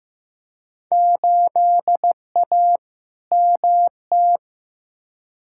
Morse code, heard as a keyed tone, about 16 wpm, 700 Hz. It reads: 8A MT
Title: Field Day Exchanges Encore - 15wpm